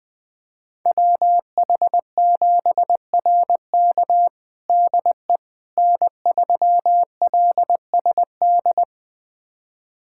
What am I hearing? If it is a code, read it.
Morse code, heard as a keyed tone, about 20 wpm, 700 Hz. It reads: WH7RK DE N3LSD
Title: USA Callsign DE Callsign - 5 Characters - 20wpm